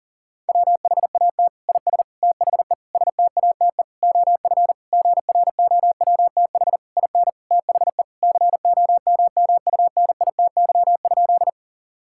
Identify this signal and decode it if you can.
Morse code, heard as a keyed tone, about 40 wpm, 700 Hz. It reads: WHAT IS THE STATE OF GROWTH IN THE COMMUNITY?
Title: Sentences from Top 1000 Words - Review Part 5 - 40wpm